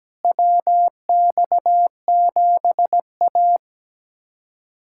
Morse code, heard as a keyed tone, about 17 wpm, 700 Hz. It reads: WX7A
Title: USA Callsign DE Callsign - 4 Characters - 17wpm